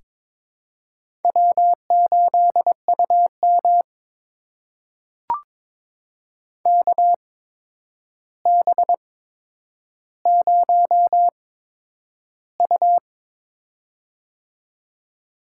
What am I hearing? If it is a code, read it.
Morse code, heard as a keyed tone, about 22 wpm, spaced as for 9 wpm, 700 Hz. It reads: W8UM KB0U
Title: USA Call Signs - 4 Characters - 8x Character Spacing - 22wpm